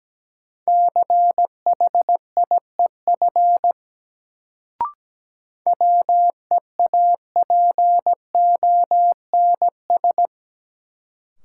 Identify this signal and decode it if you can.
Morse code, heard as a keyed tone, about 17 wpm, 700 Hz. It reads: CHIEF WEAPONS
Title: Sets of 5 Words Encore - New Word Drill - 17wpm